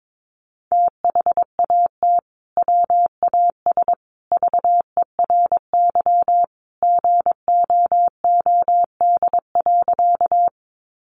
Morse code, heard as a keyed tone, about 22 wpm, 700 Hz. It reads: THAT WAS VERY GOOD.
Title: Sentences from Top 200 Words 22wpm